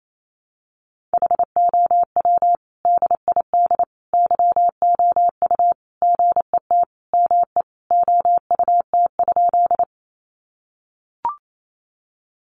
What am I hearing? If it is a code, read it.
Morse code, heard as a keyed tone, about 28 wpm, 700 Hz. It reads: HOW DID YOU GET ME OUT?
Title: Sentences from Top 200 Words 28wpm